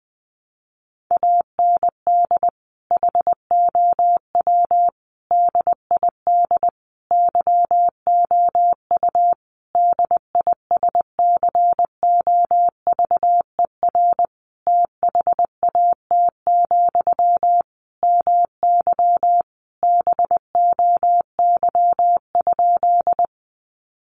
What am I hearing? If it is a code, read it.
Morse code, heard as a keyed tone, about 20 wpm, 700 Hz. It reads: AND HOW DID YOU DISCOVER THAT, MY BOY?
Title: Sentences from Top 1100 Words 20wpm